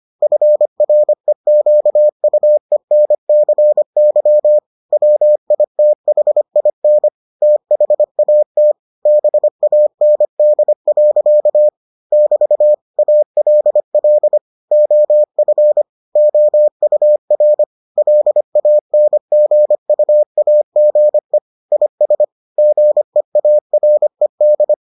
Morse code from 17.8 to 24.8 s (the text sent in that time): LANGUAGE IS GEARED